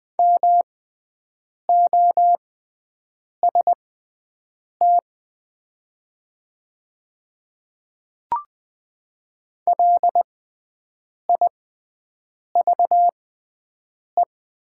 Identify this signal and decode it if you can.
Morse code, heard as a keyed tone, about 20 wpm, spaced as for 7 wpm, 700 Hz. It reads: MOST LIVE
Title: 6x Character Spacing - Top 200 Words - 20wpm